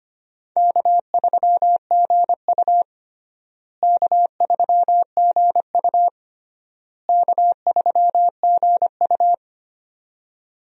Morse code, heard as a keyed tone, about 25 wpm, 700 Hz. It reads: K3GU K3GU K3GU
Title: USA Call Signs - 4 Characters - Repeated 3x - 25wpm